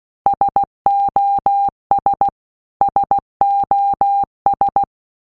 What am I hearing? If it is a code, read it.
Morse code, heard as a keyed tone, about 16 wpm, 800 Hz. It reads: SOS SOS
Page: SOS sound in Morse Code (Slow)